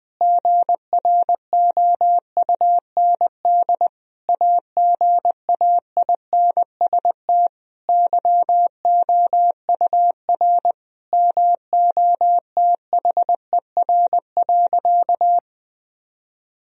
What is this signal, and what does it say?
Morse code, heard as a keyed tone, about 20 wpm, 700 Hz. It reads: GROUND AGAINST YOUR MOTHER.
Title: Sentences from Top 1000 Words - Review Part 2 - 20wpm